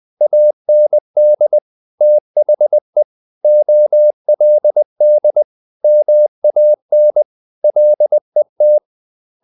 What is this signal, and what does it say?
Morse code, heard as a keyed tone, about 20 wpm, 600 Hz. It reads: AND THE OLD MAN LET
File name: the old man and the sea part 2 20wpm